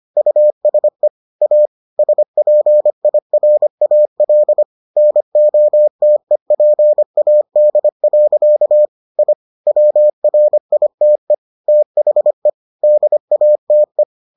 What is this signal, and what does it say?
Morse code, heard as a keyed tone, about 25 wpm, 600 Hz. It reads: USE A SPIRAL NOTEPAD. I WRITE THE DATE